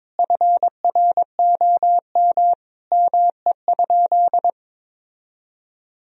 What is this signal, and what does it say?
Morse code, heard as a keyed tone, about 22 wpm, 700 Hz. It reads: FROM ME?
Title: Sentences from Top 1000 Words - Review Part 4 - 22wpm